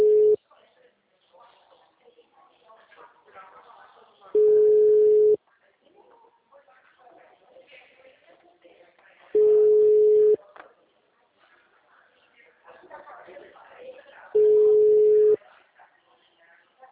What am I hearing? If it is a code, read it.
Ringback tone (Europe, 425 Hz), with speech.